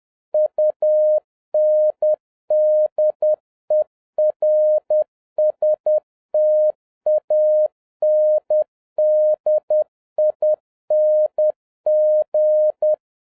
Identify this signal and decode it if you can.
Morse code, heard as a keyed tone, about 10 wpm, 610 Hz. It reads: UNDERSTANDING